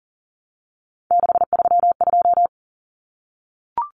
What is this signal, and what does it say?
Morse code, heard as a keyed tone, about 40 wpm, 700 Hz. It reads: 632
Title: Common Cut Numbers - 3-Digit 40wpm